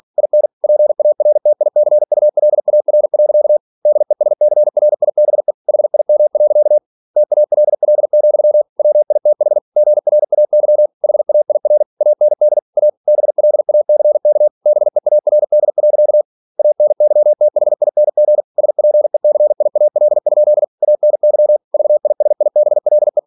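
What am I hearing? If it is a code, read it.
Morse code, heard as a keyed tone, about 46 wpm, 600 Hz. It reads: IN PARTICULAR. DESCRIBE HIM. TALL, WITH GRAY HAIR AND A BLACK BEARD. ANYTHING SPECIAL? ANY VISIBLE
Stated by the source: Practice text from an English novel sent at 47 WPM.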